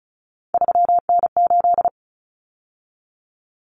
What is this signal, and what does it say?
Morse code, heard as a keyed tone, about 34 wpm, 700 Hz. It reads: 3N8
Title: Common Cut Numbers - 3-Digit 35wpm